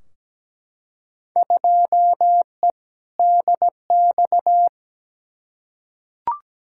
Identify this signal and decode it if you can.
Morse code, heard as a keyed tone, about 17 wpm, 700 Hz. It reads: 2E DX
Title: Field Day Exchanges - 17wpm